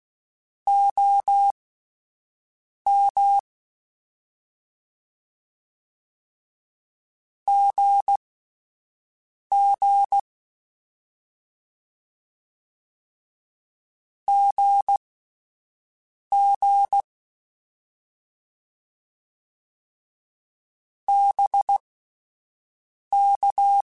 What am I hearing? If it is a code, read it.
Morse code, heard as a keyed tone, about 16 wpm, spaced as for 5 wpm, 800 Hz. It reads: OM GG GG BK
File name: Abreviations / Abreviations mp3s / Abreviations@5